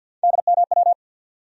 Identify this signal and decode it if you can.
Morse code, heard as a keyed tone, about 52 wpm, 700 Hz. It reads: DMW
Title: USA Call Sign Postfixes - 3 Letters 50wpm